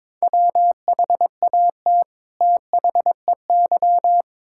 Morse code, heard as a keyed tone, about 22 wpm, 700 Hz. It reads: WHAT THEY
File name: Build A Phrase - 2-Word Phrases - 22wpm